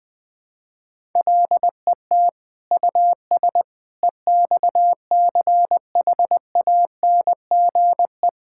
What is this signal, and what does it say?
Morse code, heard as a keyed tone, about 20 wpm, 700 Hz. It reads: LET US EXCHANGE
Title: Sentences from Top 1000 Words - Review Part 6 - 20wpm